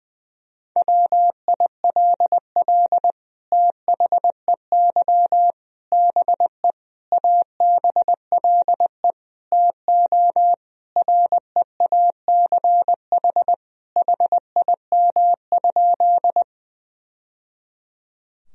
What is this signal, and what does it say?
Morse code, heard as a keyed tone, about 21 wpm, 700 Hz. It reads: WILL THEY BE ABLE TO REACH HIM?
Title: Sentences from Top 1000 Words - Review Part 2 - 20wpm